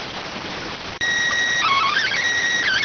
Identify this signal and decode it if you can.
SSTV (slow-scan television), PD 290.